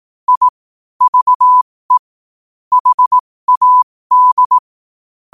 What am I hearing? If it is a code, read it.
Morse code, heard as a keyed tone, about 18 wpm, spaced as for 15 wpm, 1000 Hz. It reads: I VE HAD
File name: qotdfiles / 15wpm